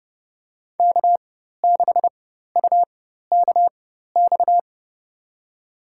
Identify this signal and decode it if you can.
Morse code, heard as a keyed tone, about 30 wpm, spaced as for 14 wpm, 700 Hz. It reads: K6UKX
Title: USA Call Signs - 5 Characters - 4x Character Spacing - 30wpm